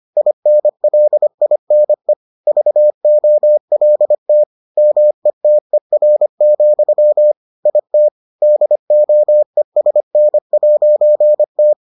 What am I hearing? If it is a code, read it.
Morse code, heard as a keyed tone, about 25 wpm, 600 Hz. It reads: INLINE VOLT METER, IT DOESN'T